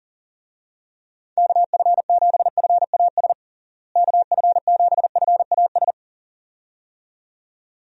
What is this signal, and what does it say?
Morse code, heard as a keyed tone, about 40 wpm, 700 Hz. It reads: KF7FAS KF7FAS
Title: USA Call Signs - 6 Characters - Repeated 2x - 40wpm